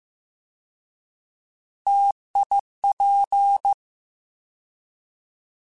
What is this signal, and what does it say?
Morse code, heard as a keyed tone, about 15 wpm, 800 Hz. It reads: TIP
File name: T Prefixes / T mp3s